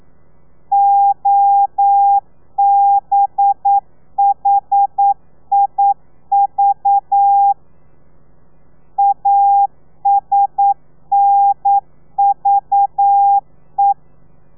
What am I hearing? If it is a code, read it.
Morse code, heard as a keyed tone, about 9 wpm, 800 Hz. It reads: OBHIV ASNVE
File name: telegrafia / i2tfj / Lez4 40